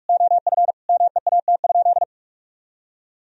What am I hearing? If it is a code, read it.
Morse code, heard as a keyed tone, about 45 wpm, 700 Hz. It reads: OF MEAT?
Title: Sentences from Top 1000 Words - Review Part 3 - 45wpm